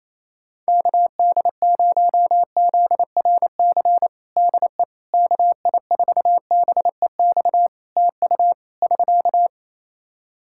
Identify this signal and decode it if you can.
Morse code, heard as a keyed tone, about 28 wpm, 700 Hz. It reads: KD0ZRC DE KI4BEX TU <SK>
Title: Practice QSO - Exchange 4 - Ending - 28wpm